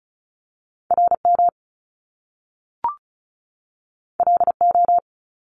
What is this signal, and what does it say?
Morse code, heard as a keyed tone, about 35 wpm, 700 Hz. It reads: RM LO
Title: Top Two Letter Combinations in Words 35wpm